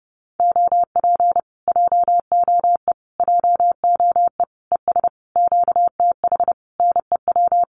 Morse code, heard as a keyed tone, about 30 wpm, 700 Hz. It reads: OP JOE JOE ES QTH NEW
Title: Common QSO Phrases Encore 30wpm